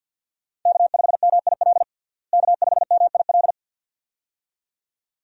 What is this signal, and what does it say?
Morse code, heard as a keyed tone, about 52 wpm, 700 Hz. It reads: K5MIL K5MIL
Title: USA Call Signs - 5 Characters - Repeated 2x - 50wpm